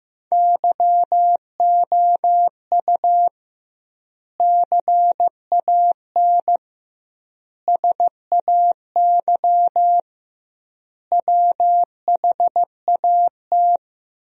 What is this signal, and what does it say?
Morse code, heard as a keyed tone, about 15 wpm, 700 Hz. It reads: YOU CAN SAY WHAT
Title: 2x Word-Spacing - Sentences from Top 200 Words - 15wpm